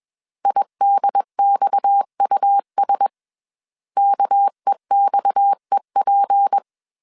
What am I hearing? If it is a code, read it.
Morse code, heard as a keyed tone, about 21 wpm, 800 Hz. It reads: IDXUS KEXEP
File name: CW / DET / 1 6 1500C S B 20wpm 2048